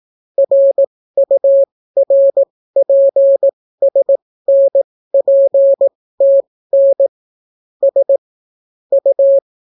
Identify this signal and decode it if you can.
Morse code, heard as a keyed tone, about 18 wpm, spaced as for 14 wpm, 550 Hz. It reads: RURPSNPTN S U